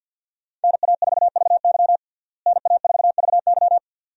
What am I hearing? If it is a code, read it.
Morse code, heard as a keyed tone, about 52 wpm, 700 Hz. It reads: NA4VY NA4VY
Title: USA Call Signs - 5 Characters - Repeated 2x - 50wpm